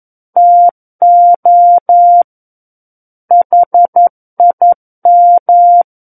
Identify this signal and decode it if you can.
Morse code, heard as a keyed tone, about 11 wpm, 700 Hz. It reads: TO HIM
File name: Book One / 02-Chapter 2 - 11 WPM